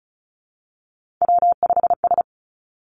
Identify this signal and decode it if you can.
Morse code, heard as a keyed tone, about 36 wpm, 700 Hz. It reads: W5S
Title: Three Challenging Letter-Number Combinations 35wpm